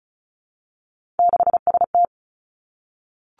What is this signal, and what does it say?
Morse code, heard as a keyed tone, about 36 wpm, 700 Hz. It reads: 6ST
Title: Three Challenging Letter-Number Combinations 35wpm